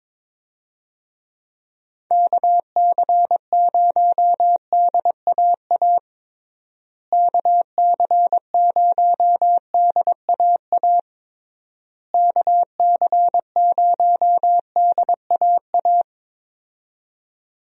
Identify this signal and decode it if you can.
Morse code, heard as a keyed tone, about 22 wpm, 700 Hz. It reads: KC0DAA KC0DAA KC0DAA
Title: USA Call Signs - 6 Characters - Repeated 3x - 22wpm